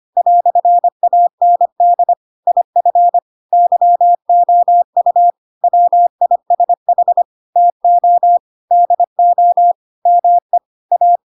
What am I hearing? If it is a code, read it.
Morse code, heard as a keyed tone, about 25 wpm, 700 Hz. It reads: "AND IF YOU WISH TO DO ME A
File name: Pinocchio / Pinocchio ch4 25 25 ik1hge